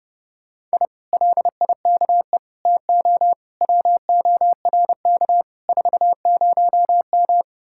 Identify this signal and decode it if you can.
Morse code, heard as a keyed tone, about 33 wpm, 700 Hz. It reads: I LIKE TO WORK 40M